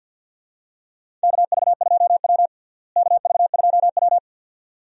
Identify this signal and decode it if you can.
Morse code, heard as a keyed tone, about 51 wpm, 700 Hz. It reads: KV2W KV2W
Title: USA Call Signs - 4 Characters - Repeated 2x - 50wpm